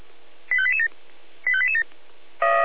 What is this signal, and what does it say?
Five-tone selective call (ZVEI), heard as quick runs of five tones, tones of about 70 ms. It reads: 85887 85887